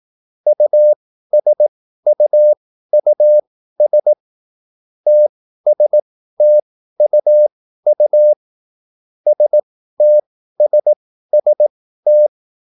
Morse code, heard as a keyed tone, about 18 wpm, spaced as for 13 wpm, 600 Hz. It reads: USUUS TSTUU STSST
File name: QSL / CW lesson